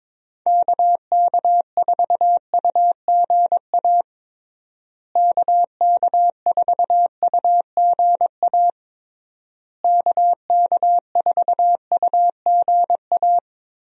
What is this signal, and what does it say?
Morse code, heard as a keyed tone, about 22 wpm, 700 Hz. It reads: KK4UGA KK4UGA KK4UGA
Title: USA Call Signs - 6 Characters - Repeated 3x - 22wpm